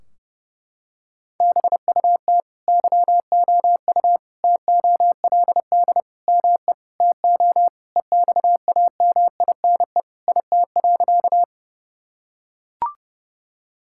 Morse code, heard as a keyed tone, about 32 wpm, 700 Hz. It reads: BUT YOU TOLD ME TO EXAMINE IT.
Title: Sentences from Top 1600 Words 30wpm